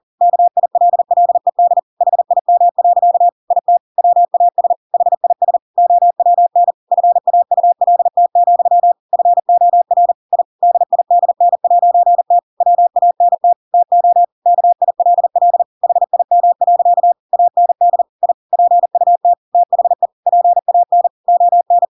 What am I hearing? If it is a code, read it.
Morse code, heard as a keyed tone, about 40 wpm, 700 Hz. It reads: KILLED HIM. IT WAS HIS OWN FAULT, FOR I DIDN'T WANT TO KILL HIM. AND I PUT THE PAN ON
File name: Pinocchio / Pinocchio ch7 40 40 ik1hge